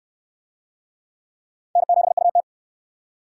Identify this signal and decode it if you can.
Morse code, heard as a keyed tone, about 68 wpm, 700 Hz. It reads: N7UT